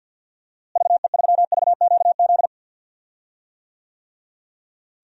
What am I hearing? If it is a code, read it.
Morse code, heard as a keyed tone, about 51 wpm, 700 Hz. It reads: VE3VQZ
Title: DXCC-100 Call Signs - 6 Characters - Repeated 2x - 50wpm